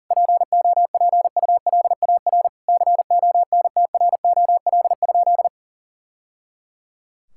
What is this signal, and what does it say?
Morse code, heard as a keyed tone, about 40 wpm, 700 Hz. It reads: POPULAR CONTROL?
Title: Sentences from Top 1000 Words - Review Part 5 - 40wpm